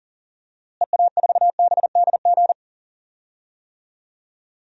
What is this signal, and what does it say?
Morse code, heard as a keyed tone, about 40 wpm, 700 Hz. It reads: EA4BDG
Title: DXCC-100 Call Signs - 6 Characters - Repeated 3x - 40wpm